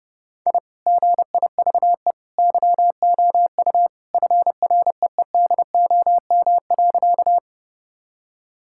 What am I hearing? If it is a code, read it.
Morse code, heard as a keyed tone, about 30 wpm, 700 Hz. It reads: I GIVE YOU FREEDOM.
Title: Sentences from Top 1000 Words - Review Part 5 - 30wpm